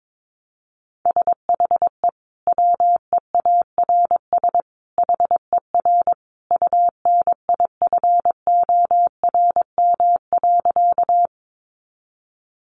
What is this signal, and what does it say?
Morse code, heard as a keyed tone, about 22 wpm, 700 Hz. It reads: SHE WEARS HER UNIFORM.
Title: Easy Phrases 22wpm